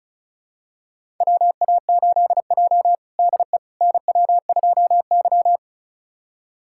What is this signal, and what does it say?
Morse code, heard as a keyed tone, about 34 wpm, 700 Hz. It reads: WA8J DE NW2Y
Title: USA Callsign DE Callsign - 4 Characters - 35wpm